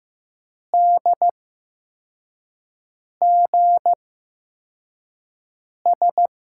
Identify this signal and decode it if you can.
Morse code, heard as a keyed tone, about 15 wpm, spaced as for 4 wpm, 700 Hz. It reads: DGS
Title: USA Call Signs - 8x Character Spacing - 15wpm